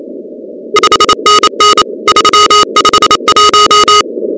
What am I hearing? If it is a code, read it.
Morse code, heard as a keyed tone, about 28 wpm, 1250 Hz, with noise. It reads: 5NN 351
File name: contests / 2019-cw-wpx-kt5j-k5pi- -2330 0-16